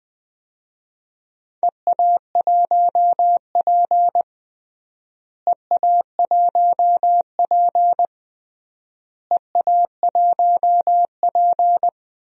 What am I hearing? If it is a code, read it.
Morse code, heard as a keyed tone, about 20 wpm, 700 Hz. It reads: EA1P EA1P EA1P
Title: DXCC-100 Call Signs - 4 Characters - Repeated 3x - 20wpm